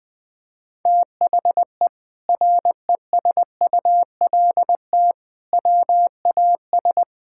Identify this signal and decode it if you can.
Morse code, heard as a keyed tone, about 20 wpm, 700 Hz. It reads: THE RESULT WAS
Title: Sentences from Top 1000 Words - Review Part 6 - 20wpm